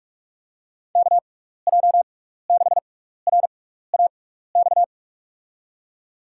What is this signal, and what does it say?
Morse code, heard as a keyed tone, about 48 wpm, spaced as for 16 wpm, 700 Hz. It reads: KJ6RAX
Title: USA Call Signs - 6 Characters - 6x Character Spacing - 45wpm